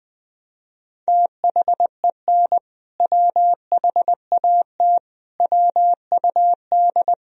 Morse code, heard as a keyed tone, about 21 wpm, 700 Hz. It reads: THEN WHAT WUD
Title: Sentences from Top 400 Words and Abbreviations 20wpm